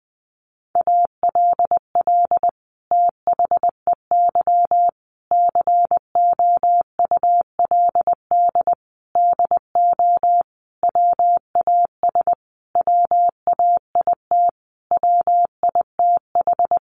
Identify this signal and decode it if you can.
Morse code, heard as a keyed tone, about 20 wpm, 700 Hz. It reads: ALL THEY COULD DO WAS WAIT WITH
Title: Sentences from Top 1300 Words 20wpm